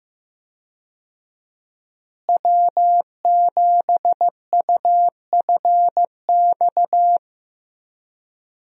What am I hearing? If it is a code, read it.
Morse code, heard as a keyed tone, about 15 wpm, 700 Hz. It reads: W7UFX
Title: USA Call Signs - 5 Characters - Repeated 2x - 15wpm